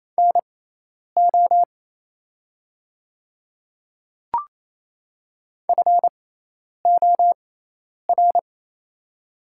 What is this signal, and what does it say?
Morse code, heard as a keyed tone, about 28 wpm, spaced as for 10 wpm, 700 Hz. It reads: NO FOR
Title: 6x Character Spacing - Top 100 Words - 28wpm